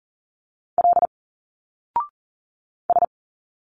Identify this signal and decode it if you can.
Morse code, heard as a keyed tone, about 40 wpm, 700 Hz. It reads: L S
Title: Single Letters 40wpm